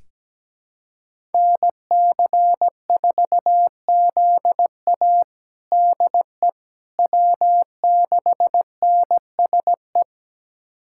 Morse code, heard as a keyed tone, about 17 wpm, 700 Hz. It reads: NC4ZA DE W6NSE
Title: USA Callsign DE Callsign - 5 Characters - 17wpm